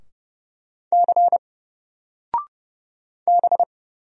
Morse code, heard as a keyed tone, about 31 wpm, 700 Hz. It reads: C B
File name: Single-Letter-Number-30wpm